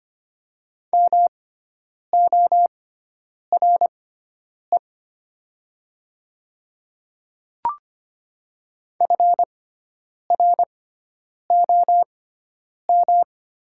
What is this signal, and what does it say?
Morse code, heard as a keyed tone, about 25 wpm, spaced as for 9 wpm, 700 Hz. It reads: MORE FROM
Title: 6x Character Spacing - Top 100 Words - 25wpm